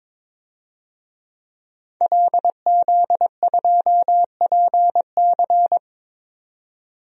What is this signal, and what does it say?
Morse code, heard as a keyed tone, about 22 wpm, 700 Hz. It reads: LZ2PC
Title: DXCC-100 Call Signs - 5 Characters - 22wpm